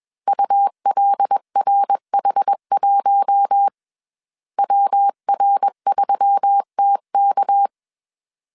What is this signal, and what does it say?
Morse code, heard as a keyed tone, about 21 wpm, 800 Hz. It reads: ULRH1 WR3TK
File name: CW / DET / 1 6 1500C S C 20wpm 2116